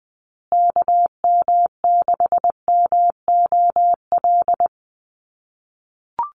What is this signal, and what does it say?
Morse code, heard as a keyed tone, about 20 wpm, 700 Hz. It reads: KM6MOL
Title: USA Call Signs Repeated 2x 20wpm